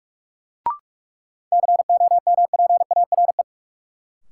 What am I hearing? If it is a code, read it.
Morse code, heard as a keyed tone, about 46 wpm, 700 Hz. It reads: COMPARE
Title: Sentences from Top 1300 Words - New Word Drill Alt - 45wpm